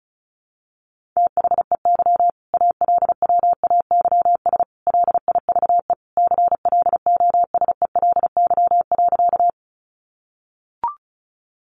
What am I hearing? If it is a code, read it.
Morse code, heard as a keyed tone, about 34 wpm, 700 Hz. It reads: THEY ALWAYS LIVE CLOSELY.
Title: Sentences from Top 1400 Words 35wpm